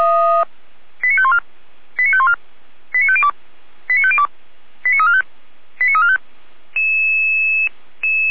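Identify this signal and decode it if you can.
Five-tone selective call (ZVEI), heard as quick runs of five tones, tones of about 70 ms. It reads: 89514 89514 89502 89502 89345 89345
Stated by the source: ZVEI five-tone call 89305 received by SDR, recorded by feuerwehr-orlamuende (F85360) fire brigade.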